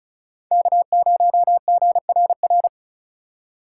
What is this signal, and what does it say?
Morse code, heard as a keyed tone, about 46 wpm, 700 Hz. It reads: K0GRR